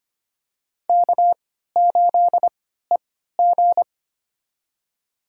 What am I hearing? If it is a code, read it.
Morse code, heard as a keyed tone, about 25 wpm, spaced as for 14 wpm, 700 Hz. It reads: K8EG
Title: USA Call Signs - 4 Characters - 3x Character Spacing - 25wpm